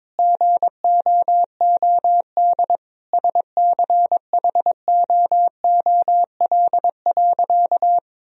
Morse code, heard as a keyed tone, about 22 wpm, 700 Hz. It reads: GOOD SCHOOL.
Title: Sentences from Top 1000 Words - Review Part 1 - 22wpm